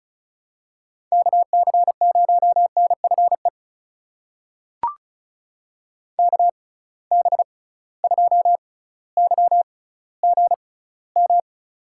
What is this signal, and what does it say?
Morse code, heard as a keyed tone, about 36 wpm, spaced as for 18 wpm, 700 Hz. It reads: KC0NFE KB2YGM
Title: USA Call Signs - 6 Characters - 6x Character Spacing - 35wpm